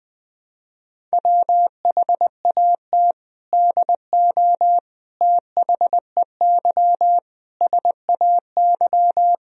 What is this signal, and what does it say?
Morse code, heard as a keyed tone, about 20 wpm, 700 Hz. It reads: WHAT DO THEY SAY
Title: Sentences from Top 1000 Words - Review Part 1 - 20wpm